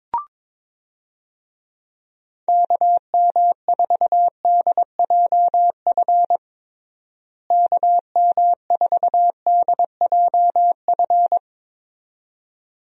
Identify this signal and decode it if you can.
Morse code, heard as a keyed tone, about 21 wpm, 700 Hz. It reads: KM4DJF KM4DJF
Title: USA Call Signs - 6 Characters - Repeated 2x - 22wpm